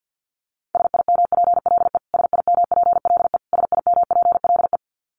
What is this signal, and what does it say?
Morse code, heard as a keyed tone, about 50 wpm, 700 Hz. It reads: SIMPLE SIMPLE SIMPLE
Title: Top 500 Words Repeated 50wpm